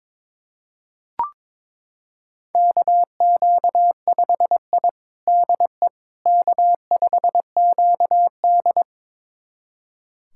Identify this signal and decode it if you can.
Morse code, heard as a keyed tone, about 22 wpm, 700 Hz. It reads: KQ5I DE K5QD
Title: USA Callsign DE Callsign - 4 Characters - 22wpm